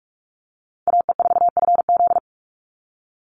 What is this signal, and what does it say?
Morse code, heard as a keyed tone, about 44 wpm, 700 Hz. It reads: AE4FZ
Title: NA SOTA Chaser Callsigns 45wpm